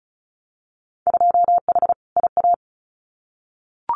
Morse code, heard as a keyed tone, about 34 wpm, 700 Hz. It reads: 2H IA
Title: Winter Field Day Exchanges Encore 35wpm